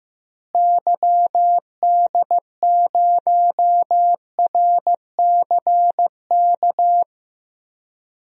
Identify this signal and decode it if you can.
Morse code, heard as a keyed tone, about 15 wpm, 700 Hz. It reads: YD0RCK
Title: DXCC-100 Call Signs - 6 Characters - Repeated 3x - 15wpm